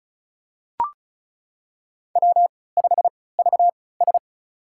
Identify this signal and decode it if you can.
Morse code, heard as a keyed tone, about 36 wpm, spaced as for 20 wpm, 690 Hz. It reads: W5VS
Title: USA Call Signs - 4 Characters - 3x Character Spacing - 35wpm